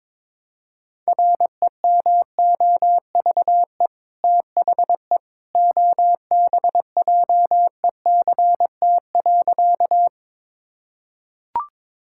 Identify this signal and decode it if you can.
Morse code, heard as a keyed tone, about 22 wpm, 700 Hz. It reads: REMOVE THE OBJECT.
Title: Sentences from Top 1000 Words - Review Part 5 - 22wpm